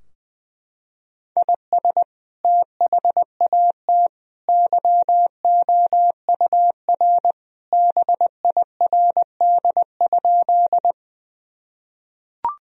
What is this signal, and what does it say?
Morse code, heard as a keyed tone, about 20 wpm, 700 Hz. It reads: IS THAT YOUR BIRD?
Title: Sentences from Top 1000 Words - Review Part 2 - 20wpm